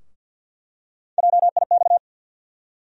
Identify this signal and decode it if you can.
Morse code, heard as a keyed tone, about 47 wpm, 700 Hz. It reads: JIX
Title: USA Call Sign Postfixes - 3 Letters 50wpm